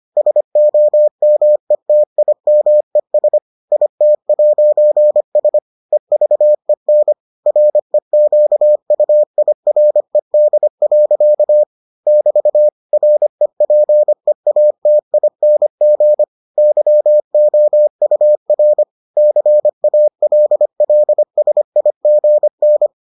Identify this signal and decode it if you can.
Morse code, heard as a keyed tone, about 25 wpm, 600 Hz. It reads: SOMETIMES IT'S EVEN REQUIRED. <BT> REPEATING YOUR CALLSIGN